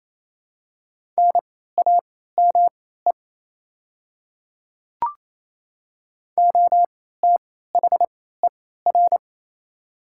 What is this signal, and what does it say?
Morse code, heard as a keyed tone, about 28 wpm, spaced as for 16 wpm, 700 Hz. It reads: NAME OTHER
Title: 3x Character Spacing - Top 300 Words - 28wpm